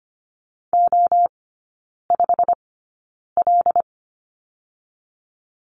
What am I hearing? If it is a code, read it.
Morse code, heard as a keyed tone, about 25 wpm, spaced as for 9 wpm, 700 Hz. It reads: O5L
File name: course / 25:10wpm / 100 Lesson 17e - Letter u - Three Characters 25wpm